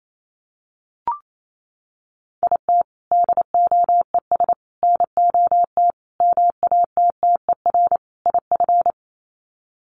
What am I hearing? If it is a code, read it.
Morse code, heard as a keyed tone, about 28 wpm, 690 Hz. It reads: IT DOES NOT MATTER IF
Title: Sets of 5 Words Encore 28wpm